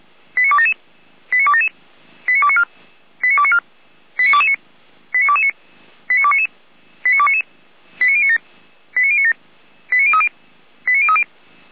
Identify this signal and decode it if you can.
Five-tone selective call (ZVEI), heard as quick runs of five tones, tones of about 70 ms. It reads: 89277 89277 89294 89294 89229 89229 89299 89299 89097 89097 89030 89030